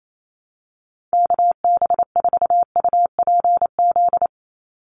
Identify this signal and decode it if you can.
Morse code, heard as a keyed tone, about 28 wpm, 700 Hz. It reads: KB4UPZ
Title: USA Call Signs 28wpm